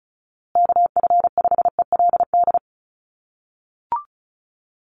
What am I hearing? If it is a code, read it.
Morse code, heard as a keyed tone, about 34 wpm, 700 Hz. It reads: KF5ELD
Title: USA Call Signs Repeated 2x 35wpm